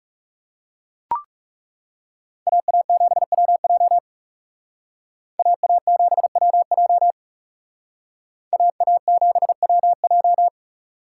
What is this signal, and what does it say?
Morse code, heard as a keyed tone, about 34 wpm, 700 Hz. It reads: AA7WJ AA7WJ AA7WJ
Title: Speed Racing - USA 2x2 Call Signs 30wpm